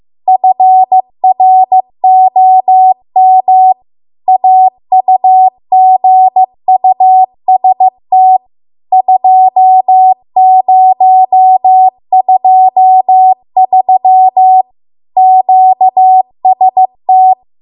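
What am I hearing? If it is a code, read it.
Morse code, heard as a keyed tone, about 15 wpm, 750 Hz. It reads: FROM AUGUST 2023 QST
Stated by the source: Morse code practice at 15 wpm from the national amateur radio association's code archive.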